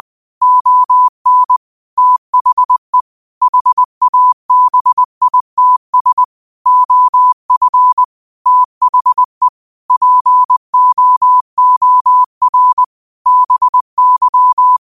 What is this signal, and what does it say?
Morse code, heard as a keyed tone, about 20 wpm, 1000 Hz. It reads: ON THE HABITS OF THE POOR BY